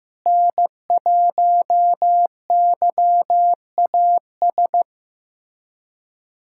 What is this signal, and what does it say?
Morse code, heard as a keyed tone, about 15 wpm, 700 Hz. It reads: N1YAS
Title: USA Callsign DE Callsign - 5 Characters - 15wpm